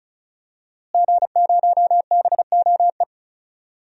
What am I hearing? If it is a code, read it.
Morse code, heard as a keyed tone, about 36 wpm, 700 Hz. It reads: G0BOE
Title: DXCC-100 Call Signs - 5 Characters - Repeated 2x - 35wpm